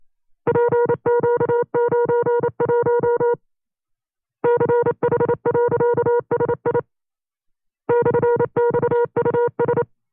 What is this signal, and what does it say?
Morse code, heard as a keyed tone, about 28 wpm, 480 Hz. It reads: PQ91 CH.SI /XUS
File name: FreeFiles / CW-practice / 25WPM-side2